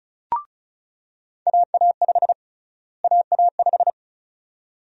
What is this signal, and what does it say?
Morse code, heard as a keyed tone, about 36 wpm, 700 Hz. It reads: AA5 AA5
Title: USA Call Sign Prefixes - Repeated 2x - 35wpm